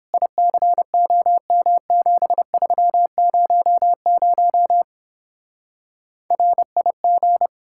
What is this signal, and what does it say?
Morse code, heard as a keyed tone, about 29 wpm, 700 Hz. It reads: ICOM7300 RIG